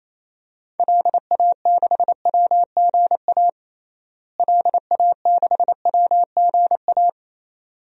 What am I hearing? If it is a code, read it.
Morse code, heard as a keyed tone, about 28 wpm, 700 Hz. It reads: LA6WGA LA6WGA
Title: DXCC-100 Call Signs - 6 Characters - Repeated 2x - 28wpm